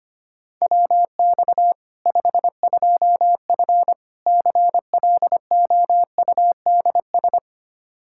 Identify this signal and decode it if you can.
Morse code, heard as a keyed tone, about 25 wpm, 700 Hz. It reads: WX 52F CLOUDS
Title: K1USN SST QSO Practice 25wpm